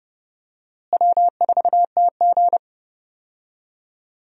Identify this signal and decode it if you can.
Morse code, heard as a keyed tone, about 32 wpm, 700 Hz. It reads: W4TG
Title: USA Call Signs - 4 Characters - Repeated 3x - 30wpm